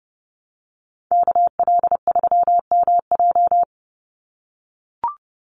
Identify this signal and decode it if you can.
Morse code, heard as a keyed tone, about 30 wpm, 700 Hz. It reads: KL3MJ
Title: USA Call Signs 30wpm